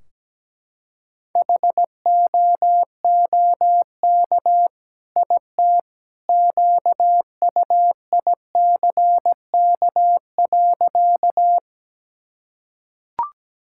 Morse code, heard as a keyed tone, about 17 wpm, 700 Hz. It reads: HOOK IT QUICK.
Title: Sentences from Top 1600 Words 17wpm